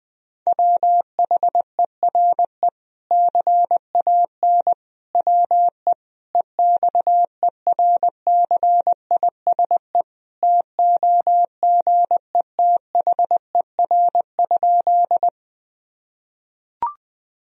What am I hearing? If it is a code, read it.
Morse code, heard as a keyed tone, about 21 wpm, 700 Hz. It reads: WHERE CAN WE EXERCISE TOGETHER?
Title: Sentences from Top 1000 Words - Review Part 4 - 20wpm